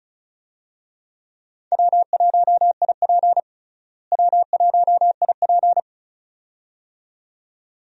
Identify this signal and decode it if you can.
Morse code, heard as a keyed tone, about 34 wpm, 700 Hz. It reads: W1IP W1IP
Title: USA Call Signs - 4 Characters - Repeated 2x - 35wpm